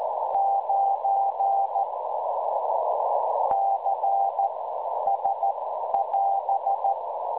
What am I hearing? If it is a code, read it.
Morse code, heard as a keyed tone, about 14 wpm, spaced as for 9 wpm, 790 Hz, with noise. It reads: 9 CS <AS>
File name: cwronde / websdr recording .6kHz